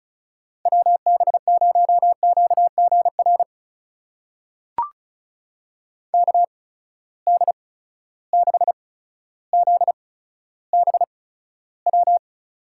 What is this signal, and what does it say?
Morse code, heard as a keyed tone, about 36 wpm, spaced as for 15 wpm, 700 Hz. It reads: WB0QGR KD6ZBW
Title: USA Call Signs - 6 Characters - 8x Character Spacing - 35wpm